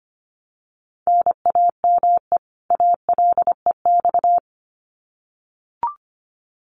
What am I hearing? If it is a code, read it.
Morse code, heard as a keyed tone, about 25 wpm, 700 Hz. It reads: NAME ALEX
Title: Common QSO Phrases 25wpm